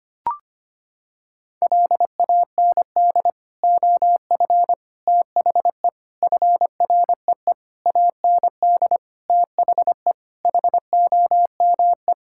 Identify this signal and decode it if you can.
Morse code, heard as a keyed tone, about 25 wpm, 700 Hz. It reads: LAND OF THE FREE AND THE HOME
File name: practice / 25wpm / Sentences with 9-Word Phrases - 25wpm